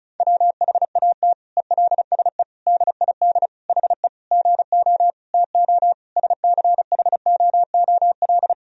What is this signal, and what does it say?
Morse code, heard as a keyed tone, about 36 wpm, 700 Hz. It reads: WHAT ELSE DID HE GO TO SCHOOL
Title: Sentences from Top 1000 Words - Review Part 1 - 35wpm